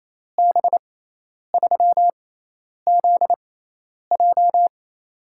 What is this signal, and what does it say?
Morse code, heard as a keyed tone, about 28 wpm, spaced as for 10 wpm, 700 Hz. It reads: B3ZJ
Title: USA Call Signs - 5 Characters - 6x Character Spacing - 28wpm